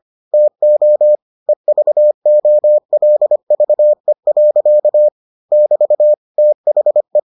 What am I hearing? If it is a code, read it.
Morse code, heard as a keyed tone, about 25 wpm, 600 Hz. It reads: TO EVOLVE. <BT> THE